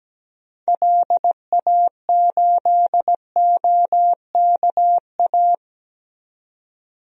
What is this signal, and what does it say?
Morse code, heard as a keyed tone, about 17 wpm, 700 Hz. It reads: LA8OKA
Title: DXCC-100 Call Signs - 6 Characters - 17wpm